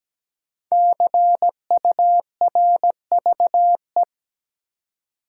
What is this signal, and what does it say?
Morse code, heard as a keyed tone, about 17 wpm, 700 Hz. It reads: CURVE
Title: Five Letter Words - 17wpm